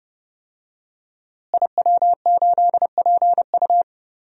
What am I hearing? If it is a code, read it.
Morse code, heard as a keyed tone, about 30 wpm, 700 Hz. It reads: IW8PU
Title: DXCC-100 Call Signs - 5 Characters - Repeated 2x - 30wpm